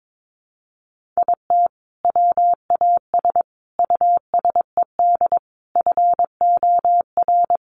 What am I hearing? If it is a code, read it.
Morse code, heard as a keyed tone, about 22 wpm, 700 Hz. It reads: IT WAS USED FOR
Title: Sentences from Top 1400 Words 22wpm